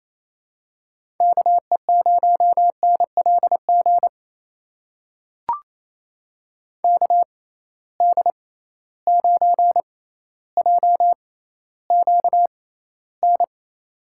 Morse code, heard as a keyed tone, about 28 wpm, spaced as for 14 wpm, 700 Hz. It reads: KE0NLG KD9JQN
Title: USA Call Signs - 6 Characters - 6x Character Spacing - 28wpm